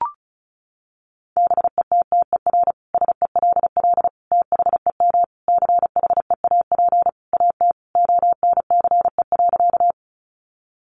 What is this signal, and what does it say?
Morse code, heard as a keyed tone, about 34 wpm, 700 Hz. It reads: BETTER SELL THEM CHEAP AT ONCE.
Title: Sentences from Top 1200 Words 35wpm